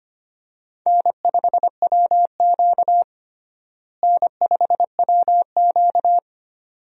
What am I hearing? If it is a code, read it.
Morse code, heard as a keyed tone, about 25 wpm, 700 Hz. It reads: N5WQ N5WQ
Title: USA Call Signs - 4 Characters - Repeated 2x - 25wpm